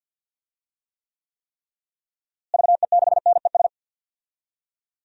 Morse code, heard as a keyed tone, about 50 wpm, 700 Hz. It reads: VE6NES
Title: DXCC-100 Call Signs - 6 Characters - Repeated 3x - 50wpm